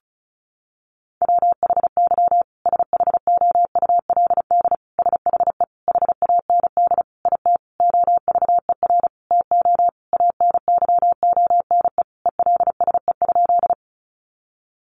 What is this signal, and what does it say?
Morse code, heard as a keyed tone, about 34 wpm, 700 Hz. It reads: WHY SHOULD SHE HAND IT OVER TO ANYONE ELSE?
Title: Sentences from Top 900 Words 35wpm